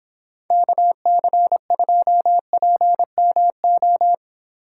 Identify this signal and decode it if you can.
Morse code, heard as a keyed tone, about 31 wpm, 700 Hz. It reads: KC2PMO